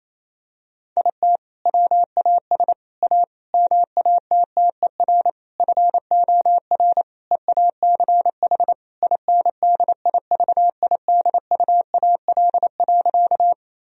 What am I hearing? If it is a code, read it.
Morse code, heard as a keyed tone, about 28 wpm, 700 Hz. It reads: IT WAS A MATTER FOR EACH INDIVIDUAL.
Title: Sentences from Top 1000 Words - Review Part 4 - 28wpm